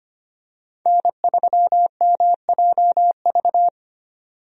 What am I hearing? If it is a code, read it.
Morse code, heard as a keyed tone, about 25 wpm, 700 Hz. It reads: N3MJV
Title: USA Call Signs - 5 Characters - 25wpm